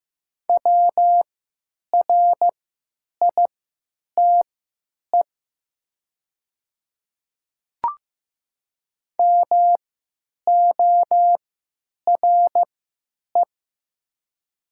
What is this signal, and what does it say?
Morse code, heard as a keyed tone, about 15 wpm, spaced as for 9 wpm, 700 Hz. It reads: WRITE MORE
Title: 3x Character Spacing - Top 100 Words - 15wpm